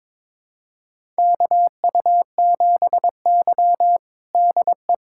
Morse code, heard as a keyed tone, about 22 wpm, 700 Hz. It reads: KU7Y DE
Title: USA Callsign DE Callsign - 4 Characters - 22wpm